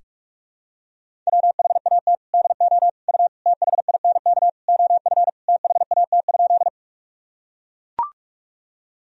Morse code, heard as a keyed tone, about 41 wpm, 700 Hz. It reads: WHAT DO U THINK OF THAT?
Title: Sentences from Top 200 Words and Abbreviations 45wpm